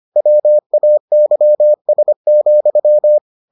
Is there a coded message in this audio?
Yes: Morse code from 0.2 to 3.2 s, heard as a keyed tone, about 25 wpm, 600 Hz. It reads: WAYS,